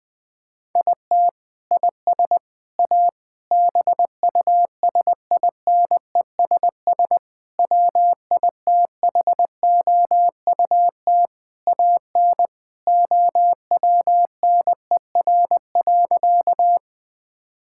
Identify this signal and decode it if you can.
Morse code, heard as a keyed tone, about 20 wpm, 700 Hz. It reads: IT IS A BUSINESS WITHOUT AN OWNER.
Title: Sentences from Top 1600 Words 20wpm